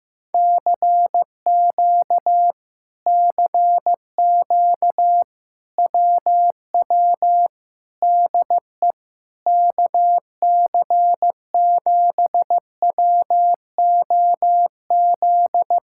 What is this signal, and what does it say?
Morse code, heard as a keyed tone, about 15 wpm, 700 Hz. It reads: CQ CQ WW DE KC7WOZ
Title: Speed-Racing Common QSO Phrases Encore - 15wpm